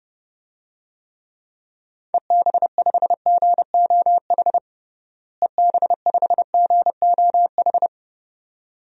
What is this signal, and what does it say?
Morse code, heard as a keyed tone, about 30 wpm, 700 Hz. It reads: EB5GOH EB5GOH
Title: DXCC-100 Call Signs - 6 Characters - Repeated 2x - 30wpm